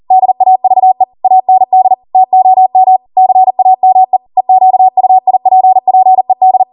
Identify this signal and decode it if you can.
Morse code, heard as a keyed tone, about 40 wpm, 750 Hz. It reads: DAVE AND TOM CAME EQUIPPED